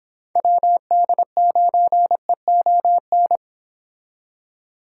Morse code, heard as a keyed tone, about 25 wpm, 700 Hz. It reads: WD9EON